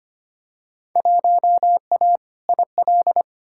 Morse code, heard as a keyed tone, about 25 wpm, 700 Hz. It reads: 1A IL
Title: Field Day Exchanges - 25wpm